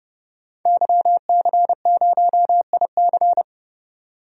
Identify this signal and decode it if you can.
Morse code, heard as a keyed tone, about 30 wpm, 700 Hz. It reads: YC0IC
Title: DXCC-100 Call Signs - 5 Characters - 30wpm